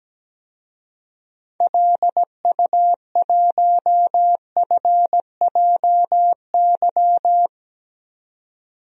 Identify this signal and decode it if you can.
Morse code, heard as a keyed tone, about 17 wpm, 700 Hz. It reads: LU1FJY
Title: DXCC-100 Call Signs - 6 Characters - 17wpm